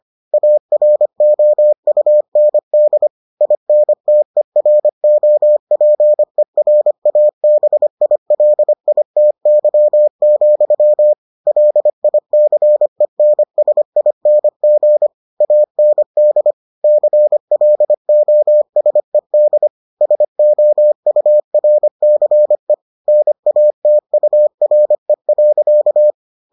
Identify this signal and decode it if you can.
Morse code, heard as a keyed tone, about 25 wpm, 600 Hz. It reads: AROUND INTEROPERABILITY, LICENSING AND CLOSED SOURCE NATURE.